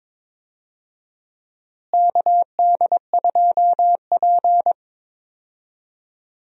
Morse code, heard as a keyed tone, about 22 wpm, 700 Hz. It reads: KD2P
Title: USA Call Signs - 4 Characters - Repeated 3x - 22wpm